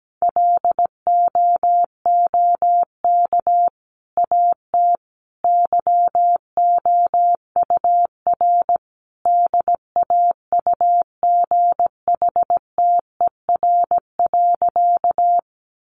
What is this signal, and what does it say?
Morse code, heard as a keyed tone, about 17 wpm, 700 Hz. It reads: LOOK AT YOUR DAUGHTER.
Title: Sentences from Top 1500 Words 17wpm